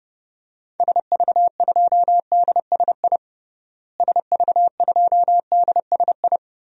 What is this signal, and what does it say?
Morse code, heard as a keyed tone, about 30 wpm, 700 Hz. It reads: SV2DSI SV2DSI
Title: DXCC-100 Call Signs - 6 Characters - Repeated 3x - 30wpm